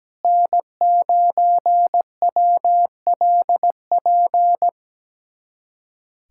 Morse code, heard as a keyed tone, about 17 wpm, 700 Hz. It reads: N9WLP
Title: USA Callsign DE Callsign - 5 Characters - 17wpm